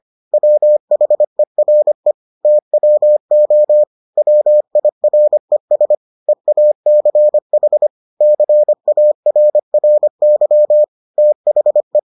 Morse code, heard as a keyed tone, about 25 wpm, 600 Hz. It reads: WHERE TWO WIRES EACH CARRY THE